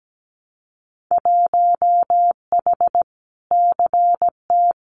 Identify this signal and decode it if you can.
Morse code, heard as a keyed tone, about 17 wpm, 700 Hz. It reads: 1H CT
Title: Winter Field Day Exchanges 17wpm